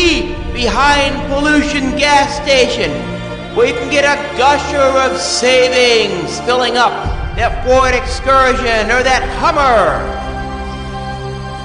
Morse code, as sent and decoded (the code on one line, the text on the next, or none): none